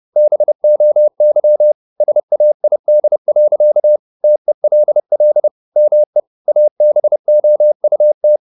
Morse code, heard as a keyed tone, about 30 wpm, 600 Hz. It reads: BOY SAID. TELL ME ABOUT
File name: the old man and the sea / the old man and the sea 1 30wpm